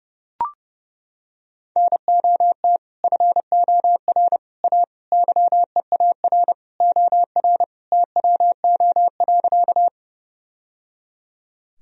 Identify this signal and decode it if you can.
Morse code, heard as a keyed tone, about 30 wpm, 700 Hz. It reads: NOT FOR A YEAR OR TWO.
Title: Sentences from Top 1000 Words - Review Part 1 - 30wpm